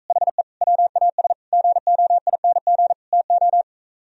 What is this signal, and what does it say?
Morse code, heard as a keyed tone, about 45 wpm, 700 Hz. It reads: HE WAS GOING TO